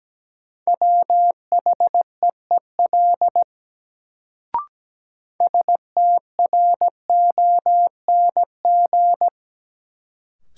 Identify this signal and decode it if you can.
Morse code, heard as a keyed tone, about 17 wpm, 700 Hz. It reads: WHEEL STRONG
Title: Sentences from Top 500 Words - New Word Drill - 17wpm